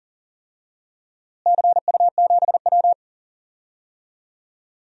Morse code, heard as a keyed tone, about 40 wpm, 700 Hz. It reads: CU7W
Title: DXCC-100 Call Signs - 4 Characters - Repeated 3x - 40wpm